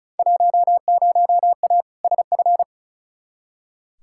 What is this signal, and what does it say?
Morse code, heard as a keyed tone, about 36 wpm, 700 Hz. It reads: 10A SF
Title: Field Day Exchanges Encore - 35wpm